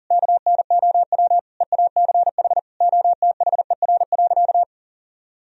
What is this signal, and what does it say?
Morse code, heard as a keyed tone, about 40 wpm, 700 Hz. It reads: KNOW EACH OTHER.
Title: Sentences from Top 1000 Words - Review Part 3 - 40wpm